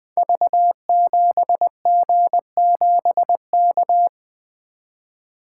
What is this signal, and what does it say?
Morse code, heard as a keyed tone, about 20 wpm, 700 Hz. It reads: V7G7K
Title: Code Groups - 5 Letter:Numbers - 20wpm